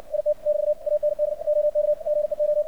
Morse code, heard as a keyed tone, about 62 wpm, 600 Hz. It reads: IT <BT> AND OK GEO
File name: archived / CRSnet